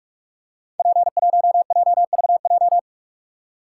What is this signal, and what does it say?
Morse code, heard as a keyed tone, about 44 wpm, 700 Hz. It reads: P1JVJ
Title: Code Groups - 5 Letter:Numbers - 45wpm